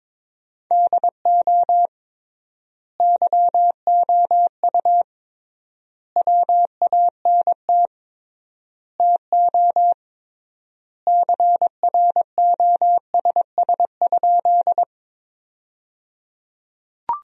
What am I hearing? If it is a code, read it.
Morse code, heard as a keyed tone, about 22 wpm, 700 Hz. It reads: DO YOU WANT TO CROSS?
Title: 3x Word-Spacing - Sentences from Top 300 Words - 22wpm